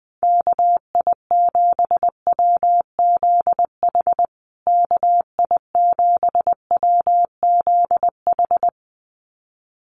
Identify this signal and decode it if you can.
Morse code, heard as a keyed tone, about 20 wpm, 700 Hz. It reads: KI7WZH KI7WZH
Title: USA Call Signs Repeated 2x 20wpm